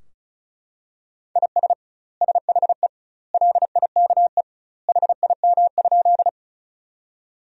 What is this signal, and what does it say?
Morse code, heard as a keyed tone, about 36 wpm, 700 Hz. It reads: IS SHE LIKE HIM?
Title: 2x Word-Spacing - Sentences from Top 100 Words - 35wpm